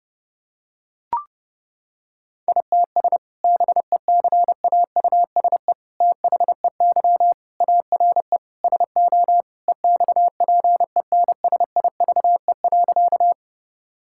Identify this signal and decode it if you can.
Morse code, heard as a keyed tone, about 30 wpm, 700 Hz. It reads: ITS BECAUSE THEY ARE SO EXPENSIVE.
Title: Sentences from Top 1000 Words - Review Part 6 - 30wpm